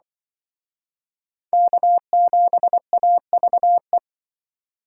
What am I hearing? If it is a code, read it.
Morse code, heard as a keyed tone, about 25 wpm, 700 Hz. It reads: K7AVE
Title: Speed Racing - USA 1x3 Call Signs 20wpm